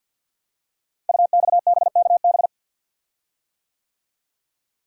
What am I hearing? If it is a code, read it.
Morse code, heard as a keyed tone, about 46 wpm, 700 Hz. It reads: UXBKB
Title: Code Groups - 5 Letters - 50wpm